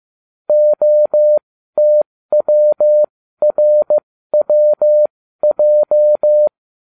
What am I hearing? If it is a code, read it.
Morse code, heard as a keyed tone, about 15 wpm, spaced as for 13 wpm, 600 Hz. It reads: OTWRWJ